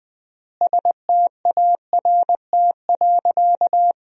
Morse code, heard as a keyed tone, about 21 wpm, 700 Hz. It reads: START.
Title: 3x Word-Spacing - Sentences from Top 300 Words - 20wpm